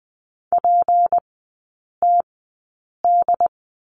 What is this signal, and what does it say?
Morse code, heard as a keyed tone, about 20 wpm, 700 Hz. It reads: P T D
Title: 178 Lesson 29e - Slash - Three Characters 20wpm